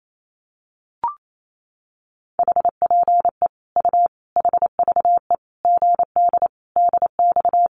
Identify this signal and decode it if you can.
Morse code, heard as a keyed tone, about 28 wpm, 700 Hz. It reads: HPE U HVE GD DX
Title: Common QSO Phrases 28wpm